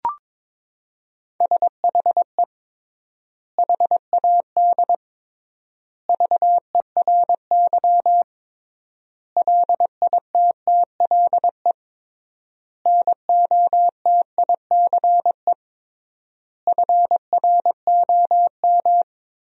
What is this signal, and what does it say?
Morse code, heard as a keyed tone, about 21 wpm, 700 Hz. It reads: SHE HAD VERY LITTLE NOTICE FROM
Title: 3x Word-Spacing - Sentences from Top 400 Words - 22wpm